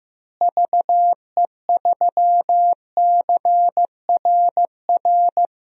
Morse code, heard as a keyed tone, about 15 wpm, 700 Hz. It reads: VE3CRR
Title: DXCC-100 Call Signs - 6 Characters - Repeated 3x - 15wpm